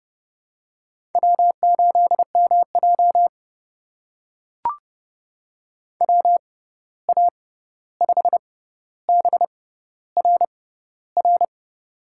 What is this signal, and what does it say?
Morse code, heard as a keyed tone, about 30 wpm, spaced as for 14 wpm, 700 Hz. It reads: W8MJ WA5BRR
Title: USA Call Signs - 6x Character Spacing - 30wpm